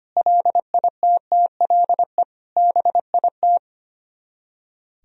Morse code, heard as a keyed tone, about 32 wpm, 700 Hz. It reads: LITTLE BIT